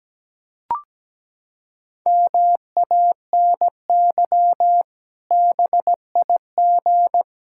Morse code, heard as a keyed tone, about 17 wpm, 700 Hz. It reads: MANY BIG
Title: Pangram Challenge v2 - 17wpm